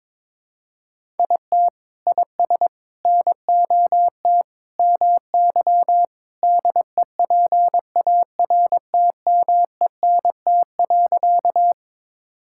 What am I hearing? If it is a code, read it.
Morse code, heard as a keyed tone, about 22 wpm, 700 Hz. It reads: IT IS NOT MY DEPARTMENT.
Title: Sentences from Top 1000 Words - Review Part 5 - 22wpm